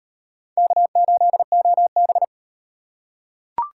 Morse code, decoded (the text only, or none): K8OB